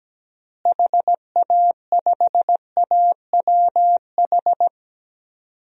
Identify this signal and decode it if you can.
Morse code, heard as a keyed tone, about 17 wpm, 700 Hz. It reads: HA5AWH
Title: DXCC-100 Call Signs - 6 Characters - Repeated 3x - 17wpm